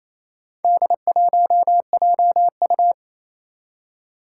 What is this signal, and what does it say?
Morse code, heard as a keyed tone, about 28 wpm, 700 Hz. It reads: D1JU
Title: Four Challenging Letter-Number Combinations 28wpm